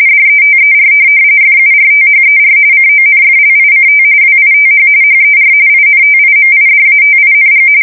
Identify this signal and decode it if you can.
RTTY, about 45.45 baud. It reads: DETER FURTHER CONFLICT WITH HER INTIMIDATING
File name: fsk170 / USS-New-Jersey